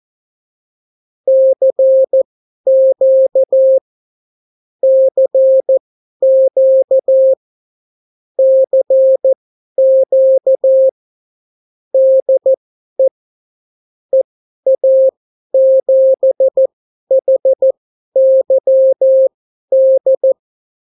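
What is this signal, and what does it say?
Morse code, heard as a keyed tone, about 14 wpm, spaced as for 11 wpm, 530 Hz. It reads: CQ CQ CQ DE EA7HYD
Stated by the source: Example Spanish CW QSO from a Morse learning book, 14 wpm characters with 8 wpm spacing.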